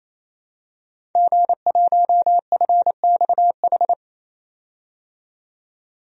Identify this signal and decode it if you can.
Morse code, heard as a keyed tone, about 28 wpm, 700 Hz. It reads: G1FXH
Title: DXCC-100 Call Signs - 5 Characters - 28wpm